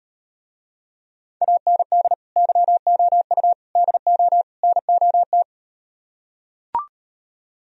Morse code, heard as a keyed tone, about 38 wpm, 700 Hz. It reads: AND YOU DO NOT